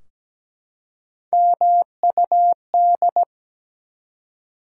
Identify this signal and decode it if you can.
Morse code, heard as a keyed tone, about 17 wpm, 700 Hz. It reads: MUD
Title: Three Letter Words - 17wpm